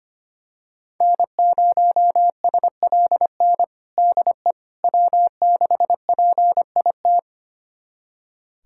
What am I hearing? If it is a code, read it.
Morse code, heard as a keyed tone, about 25 wpm, 700 Hz. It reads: N0SLN DE W6PIT
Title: USA Callsign DE Callsign - 5 Characters - 25wpm